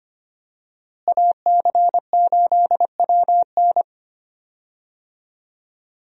Morse code, heard as a keyed tone, about 25 wpm, 700 Hz. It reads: AC8WN
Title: USA Call Signs - 5 Characters - Repeated 2x - 25wpm